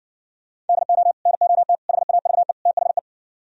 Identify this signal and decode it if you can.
Morse code, heard as a keyed tone, about 59 wpm, 700 Hz. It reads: DO NOT HAVE THE